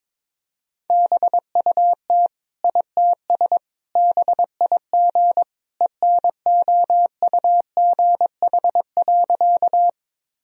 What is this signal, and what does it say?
Morse code, heard as a keyed tone, about 22 wpm, 700 Hz. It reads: BUT ITS BIG ENOUGH.
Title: Sentences from Top 1000 Words - Review Part 1 - 22wpm